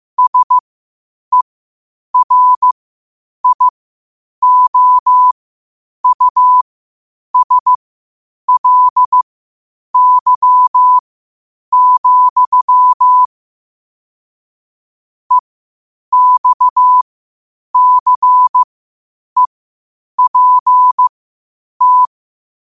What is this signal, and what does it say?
Morse code, heard as a keyed tone, about 15 wpm, spaced as for 8 wpm, 1000 Hz. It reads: SERIOUSLY, EXCEPT
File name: qotdfiles / 8wpm